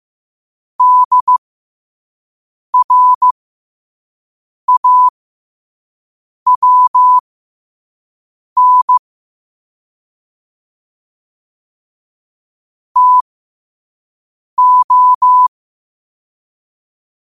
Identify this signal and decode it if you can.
Morse code, heard as a keyed tone, about 15 wpm, spaced as for 5 wpm, 1000 Hz. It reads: DRAWN TO
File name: qotdfiles / 5wpm